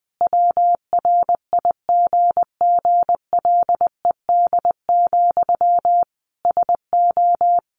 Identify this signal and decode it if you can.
Morse code, heard as a keyed tone, about 20 wpm, 700 Hz. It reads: WRIGGLED, SO
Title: A Sound Of Thunder 20wpm